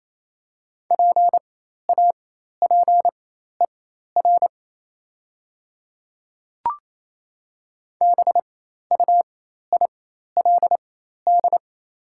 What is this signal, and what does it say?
Morse code, heard as a keyed tone, about 28 wpm, spaced as for 13 wpm, 690 Hz. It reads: PAPER BUILD
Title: 4x Character Spacing - Top 500 Words - 28wpm